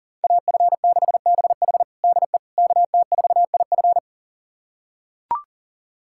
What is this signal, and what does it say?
Morse code, heard as a keyed tone, about 40 wpm, 700 Hz. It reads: AF6BH DE KT4IF
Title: USA Callsign DE Callsign - 5 Characters - 40wpm